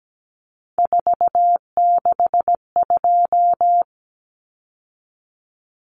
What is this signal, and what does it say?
Morse code, heard as a keyed tone, about 17 wpm, 700 Hz. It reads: 462
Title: Common Cut Numbers - 3-Digit 17wpm